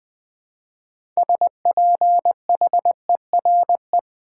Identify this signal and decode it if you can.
Morse code, heard as a keyed tone, about 20 wpm, 700 Hz. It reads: SPHERE
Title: Build A Word - 2 Letters At A Time - 6-Letter Words - 20wpm